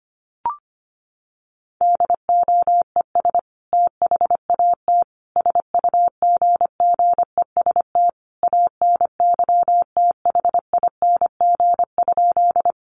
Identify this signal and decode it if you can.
Morse code, heard as a keyed tone, about 25 wpm, 700 Hz. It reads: DOES THAT SUGGEST ANYTHING?
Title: Sentences from Top 1300 Words 25wpm